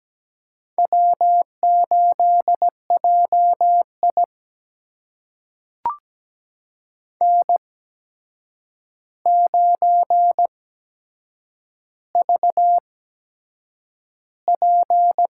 Morse code, heard as a keyed tone, about 17 wpm, spaced as for 7 wpm, 700 Hz. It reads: W8JI N9VP
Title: USA Call Signs - 4 Characters - 8x Character Spacing - 17wpm